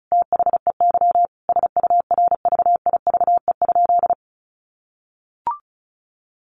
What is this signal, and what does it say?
Morse code, heard as a keyed tone, about 34 wpm, 700 Hz. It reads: THEY SURVIVE?
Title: Sentences from Top 1300 Words 35wpm